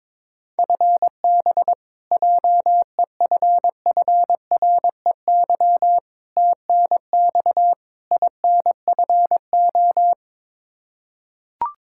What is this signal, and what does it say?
Morse code, heard as a keyed tone, about 21 wpm, 700 Hz. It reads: FB JEFFREY TNX INFO
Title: K1USN SST QSO Practice 22wpm